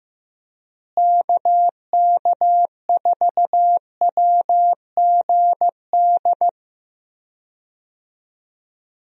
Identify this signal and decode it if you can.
Morse code, heard as a keyed tone, about 15 wpm, 700 Hz. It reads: KK4WGD
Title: USA Call Signs - 6 Characters - Repeated 3x - 15wpm